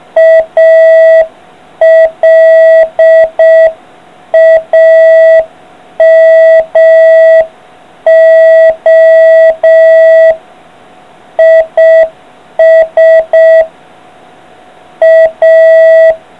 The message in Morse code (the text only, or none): ALAMO IS A